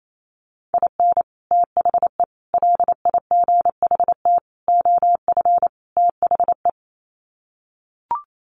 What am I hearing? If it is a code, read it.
Morse code, heard as a keyed tone, about 28 wpm, 700 Hz. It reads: IN THE LIGHT OF THE
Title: Sets of 5 Words Encore2 28wpm